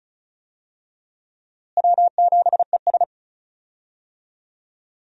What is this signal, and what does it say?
Morse code, heard as a keyed tone, about 36 wpm, 700 Hz. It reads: W7ES
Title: USA Call Signs - 4 Characters - Repeated 2x - 35wpm